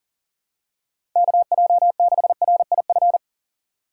Morse code, heard as a keyed tone, about 40 wpm, 700 Hz. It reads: KJ6RIF
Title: USA Call Signs - 6 Characters - 6x Character Spacing - 40wpm